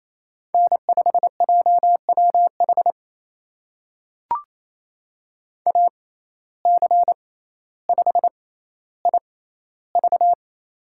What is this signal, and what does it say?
Morse code, heard as a keyed tone, about 28 wpm, spaced as for 14 wpm, 700 Hz. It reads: N5JWH AC5IV
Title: USA Call Signs - 5 Characters - 6x Character Spacing - 28wpm